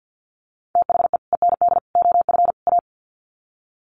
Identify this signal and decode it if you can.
Morse code, heard as a keyed tone, about 50 wpm, 700 Hz. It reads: THE END OF A
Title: Sets of 4 Words 50wpm